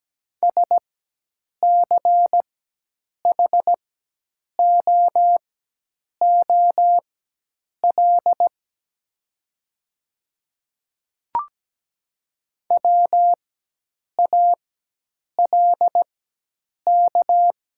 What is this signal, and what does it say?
Morse code, heard as a keyed tone, about 17 wpm, spaced as for 8 wpm, 700 Hz. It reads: SCHOOL WALK
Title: 4x Character Spacing - Top 500 Words - 17wpm